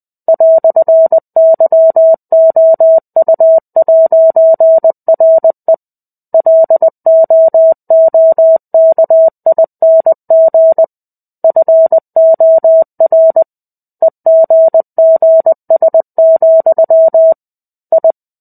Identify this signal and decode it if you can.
Morse code, heard as a keyed tone, about 20 wpm, 650 Hz. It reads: "YOU'RE LOOKING FOR EGGS, I
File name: 05-Chapter-05 20 WPM